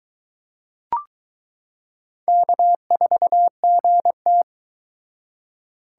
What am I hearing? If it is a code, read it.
Morse code, heard as a keyed tone, about 22 wpm, 700 Hz. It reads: K4GT